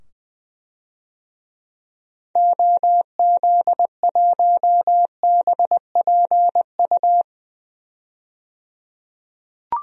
Morse code, heard as a keyed tone, about 20 wpm, 700 Hz. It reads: OZ1BPU
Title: DXCC-100 Call Signs - 6 Characters - 20wpm